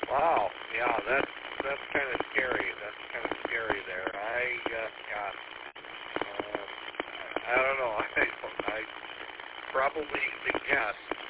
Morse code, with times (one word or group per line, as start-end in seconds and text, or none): none